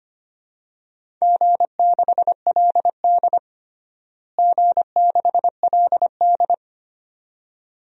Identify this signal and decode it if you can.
Morse code, heard as a keyed tone, about 25 wpm, 700 Hz. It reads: G6LD G6LD
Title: DXCC-100 Call Signs - 4 Characters - Repeated 2x - 25wpm